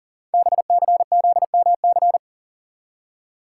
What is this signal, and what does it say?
Morse code, heard as a keyed tone, about 39 wpm, 700 Hz. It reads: BCZMC
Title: Code Groups - 5 Letters - 40wpm